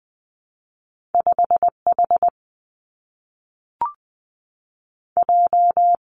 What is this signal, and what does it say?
Morse code, heard as a keyed tone, about 20 wpm, 700 Hz. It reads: 5 H J
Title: Two Challenging Letter-Number Combinations 20wpm